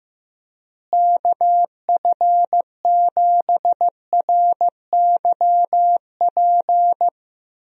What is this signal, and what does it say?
Morse code, heard as a keyed tone, about 15 wpm, 700 Hz. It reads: KF7RYP
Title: USA Call Signs - 6 Characters - Repeated 3x - 15wpm